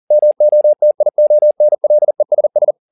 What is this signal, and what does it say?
Morse code, heard as a keyed tone, about 40 wpm, 600 Hz. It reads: MOTIONLESS
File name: the old man and the sea 2 40wpm